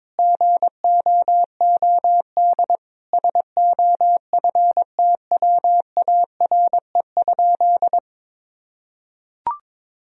Morse code, heard as a keyed tone, about 22 wpm, 700 Hz. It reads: GOOD SOFTWARE?
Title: Sentences from Top 1000 Words - Review Part 3 - 22wpm